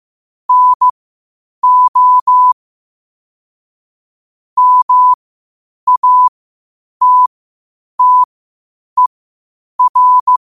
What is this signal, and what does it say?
Morse code, heard as a keyed tone, about 15 wpm, spaced as for 8 wpm, 1000 Hz. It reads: NO MATTER
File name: qotdfiles / 8wpm